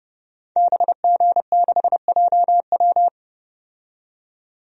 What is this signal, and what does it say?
Morse code, heard as a keyed tone, about 32 wpm, 700 Hz. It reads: BG6JW
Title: DXCC-100 Call Signs - 5 Characters - 30wpm